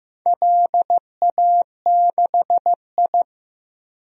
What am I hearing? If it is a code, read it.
Morse code, heard as a keyed tone, about 15 wpm, 700 Hz. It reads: LA6I
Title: DXCC-100 Call Signs - 4 Characters - Repeated 3x - 15wpm